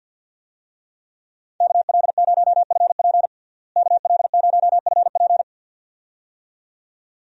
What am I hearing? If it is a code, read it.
Morse code, heard as a keyed tone, about 53 wpm, 700 Hz. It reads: KL0FP KL0FP
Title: USA Call Signs - 5 Characters - Repeated 2x - 50wpm